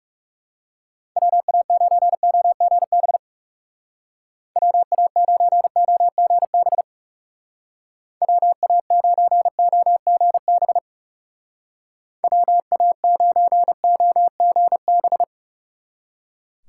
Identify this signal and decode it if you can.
Morse code, heard as a keyed tone, about 36 wpm, 700 Hz. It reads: WA9OGB WA9OGB WA9OGB WA9OGB
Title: Speed Racing - USA 2x3 Call Signs 30wpm